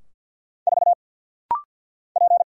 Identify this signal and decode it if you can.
Morse code, heard as a keyed tone, about 50 wpm, 690 Hz. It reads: 4 P
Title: Single Letter-Number Alt - Rapid-Fire 50wpm